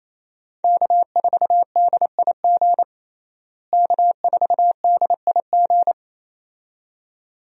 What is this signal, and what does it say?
Morse code, heard as a keyed tone, about 28 wpm, 700 Hz. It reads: K4DIG K4DIG
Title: USA Call Signs - 5 Characters - Repeated 2x - 28wpm